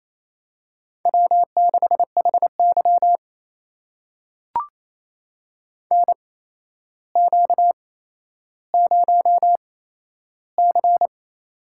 Morse code, heard as a keyed tone, about 27 wpm, spaced as for 12 wpm, 700 Hz. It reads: W6HY NQ0C
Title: USA Call Signs - 4 Characters - 8x Character Spacing - 28wpm